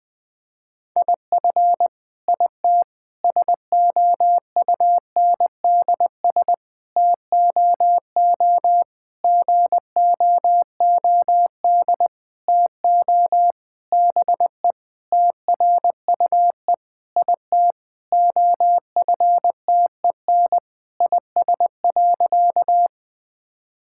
Morse code, heard as a keyed tone, about 20 wpm, 700 Hz. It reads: IF IT SOUNDS TOO GOOD TO BE TRUE IT OFTEN IS.
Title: Sentences with 9-Word Phrases - 20wpm